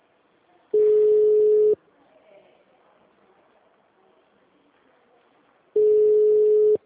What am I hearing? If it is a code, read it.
Ringback tone (Europe, 425 Hz).
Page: Dracena 2 - SP